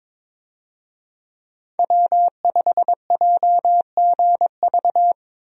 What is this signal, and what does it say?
Morse code, heard as a keyed tone, about 22 wpm, 700 Hz. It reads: W5JGV
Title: USA Call Signs - 5 Characters - Repeated 2x - 22wpm